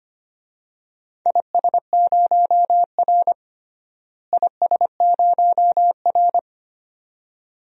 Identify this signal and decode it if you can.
Morse code, heard as a keyed tone, about 25 wpm, 700 Hz. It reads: IS0R IS0R
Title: DXCC-100 Call Signs - 4 Characters - Repeated 2x - 25wpm